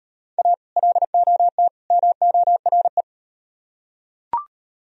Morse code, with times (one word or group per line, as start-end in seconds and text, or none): 0.4-0.5 A
0.8-1.7 LOT
1.9-3.0 MORE